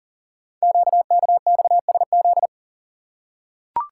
Morse code, heard as a keyed tone, about 43 wpm, 700 Hz. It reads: QKXSZ
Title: Code Groups - 5 Letters - 40wpm